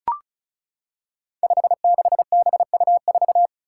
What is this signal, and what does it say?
Morse code, heard as a keyed tone, about 36 wpm, 700 Hz. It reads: 56BU4
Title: Five Challenging Letter-Number Combinations 35wpm